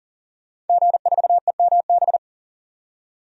Morse code, heard as a keyed tone, about 37 wpm, 700 Hz. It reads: G4EMB
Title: DXCC-100 Call Signs - 5 Characters - 40wpm